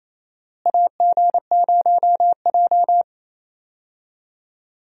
Morse code, heard as a keyed tone, about 28 wpm, 700 Hz. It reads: AG0J
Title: USA Call Signs - 4 Characters - Repeated 2x - 28wpm